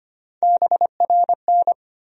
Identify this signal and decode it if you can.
Morse code, heard as a keyed tone, about 25 wpm, 700 Hz. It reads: BRN
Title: USA Call Sign Postfixes - 3 Letters 25wpm